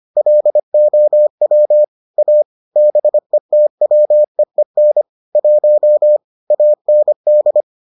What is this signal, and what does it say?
Morse code, heard as a keyed tone, about 25 wpm, 600 Hz. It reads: LOW A BETWEEN 1 AND